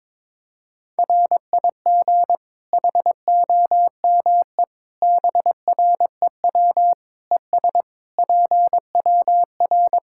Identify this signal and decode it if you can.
Morse code, heard as a keyed tone, about 22 wpm, 700 Hz. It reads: RIG HOME BREW ES PWR
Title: Practice QSO - Exchange 2 - Info - 22wpm